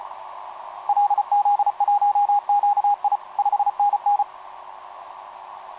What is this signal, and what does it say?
Morse code, heard as a keyed tone, about 31 wpm, 820 Hz. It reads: LZ1QI 5NN
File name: jt5dx f5idj etc 12m cw 18mar2011 0915